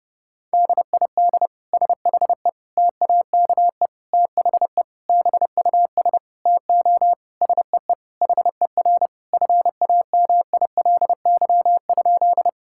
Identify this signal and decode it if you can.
Morse code, heard as a keyed tone, about 30 wpm, 700 Hz. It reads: DID SHE TAKE THE BUS TO SEE HER FAMILY?
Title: Sentences from Top 1000 Words - Review Part 6 - 30wpm